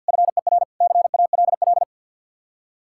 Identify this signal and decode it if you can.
Morse code, heard as a keyed tone, about 66 wpm, spaced as for 58 wpm, 700 Hz. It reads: FER CALL
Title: Speed-Racing Common QSO Phrases - 50wpm